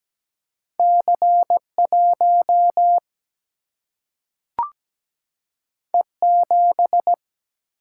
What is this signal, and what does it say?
Morse code, heard as a keyed tone, about 17 wpm, 700 Hz. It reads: C 1 E 7
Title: Single-Letter Single-Number 17wpm